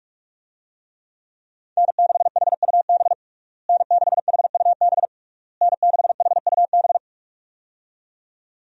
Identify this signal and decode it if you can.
Morse code, heard as a keyed tone, about 46 wpm, 700 Hz. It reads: N6HUB N6HUB N6HUB
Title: USA Call Signs - 5 Characters - Repeated 3x - 45wpm